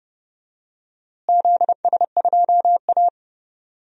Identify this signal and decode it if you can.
Morse code, heard as a keyed tone, about 30 wpm, 700 Hz. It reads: ZS2A
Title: DXCC-100 Call Signs - 4 Characters - Repeated 2x - 30wpm